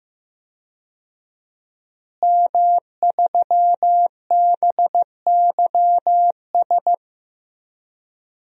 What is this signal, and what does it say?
Morse code, heard as a keyed tone, about 16 wpm, 700 Hz. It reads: M3BYS
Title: DXCC-100 Call Signs - 5 Characters - Repeated 2x - 15wpm